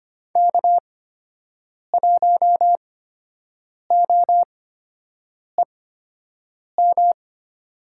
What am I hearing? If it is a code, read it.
Morse code, heard as a keyed tone, about 25 wpm, spaced as for 7 wpm, 700 Hz. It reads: K1OEM
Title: USA Call Signs - 8x Character Spacing - 25wpm